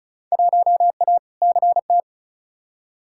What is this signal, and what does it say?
Morse code, heard as a keyed tone, about 36 wpm, 690 Hz. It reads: 1A CT
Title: Field Day Exchanges Encore - 35wpm